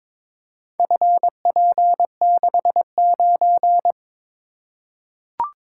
Morse code, heard as a keyed tone, about 22 wpm, 700 Hz. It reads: FP69
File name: practice / 22wpm / Four Letter-Number Combinations - 22wpm